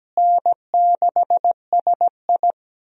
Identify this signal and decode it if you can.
Morse code, heard as a keyed tone, about 17 wpm, 700 Hz. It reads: N6SI
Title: USA Call Signs - 4 Characters - Repeated 3x - 17wpm